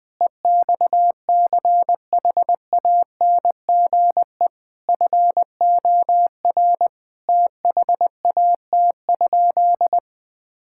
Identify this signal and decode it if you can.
Morse code, heard as a keyed tone, about 20 wpm, 700 Hz. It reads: EXCHANGE FOR THAT?
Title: Sentences from Top 1000 Words - Review Part 6 - 20wpm